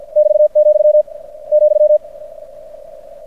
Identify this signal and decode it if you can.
Morse code, heard as a keyed tone, about 50 wpm, 600 Hz. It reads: <BT>, ,
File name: archived / CRSnet